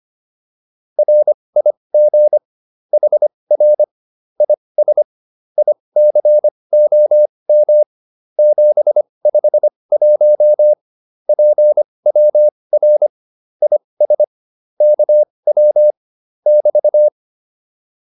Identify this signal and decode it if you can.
Morse code, heard as a keyed tone, about 25 wpm, spaced as for 20 wpm, 600 Hz. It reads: RIG HR IS ICOM 751 PWR IS KW <BT>
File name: QSO210a 20